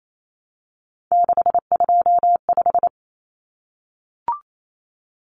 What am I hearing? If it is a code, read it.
Morse code, heard as a keyed tone, about 28 wpm, 700 Hz. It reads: 625
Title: Common Cut Numbers - 3-Digit 28wpm